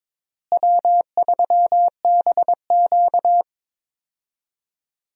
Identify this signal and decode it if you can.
Morse code, heard as a keyed tone, about 22 wpm, 700 Hz. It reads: W3BQ
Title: USA Call Signs - 4 Characters - Repeated 2x - 22wpm